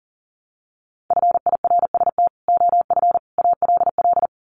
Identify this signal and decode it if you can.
Morse code, heard as a keyed tone, about 40 wpm, 700 Hz. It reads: FIRST OF ALL
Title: Sets of 3 Words 40wpm